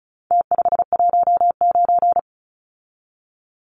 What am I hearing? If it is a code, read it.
Morse code, heard as a keyed tone, about 34 wpm, 700 Hz. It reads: T519
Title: Numbers - Contest Style - 4-Digit 35wpm